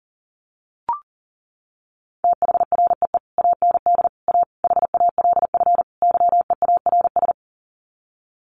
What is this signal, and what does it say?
Morse code, heard as a keyed tone, about 40 wpm, 700 Hz. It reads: THREE AND A HALF YEARS
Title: Sets of 5 Words Encore2 40wpm